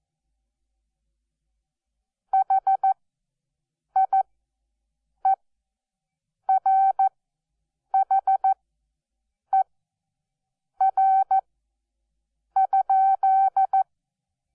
Morse code, heard as a keyed tone, about 14 wpm, spaced as for 6 wpm, 770 Hz. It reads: HIERHER?
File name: Onlinelehrgang / morsen / DJ4UF Morsen12